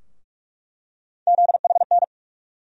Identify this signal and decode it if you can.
Morse code, heard as a keyed tone, about 46 wpm, 700 Hz. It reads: ZHN
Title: USA Call Sign Postfixes - 3 Letters 45wpm